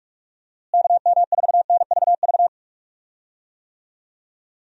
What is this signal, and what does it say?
Morse code, heard as a keyed tone, about 46 wpm, 700 Hz. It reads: KM4NVV
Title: USA Call Signs - 6 Characters - Repeated 2x - 45wpm